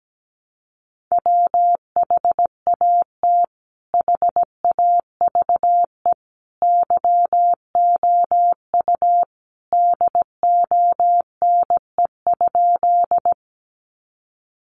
Morse code, heard as a keyed tone, about 17 wpm, 700 Hz. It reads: WHAT HAVE YOU DONE?
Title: Sentences from Top 500 Words 17wpm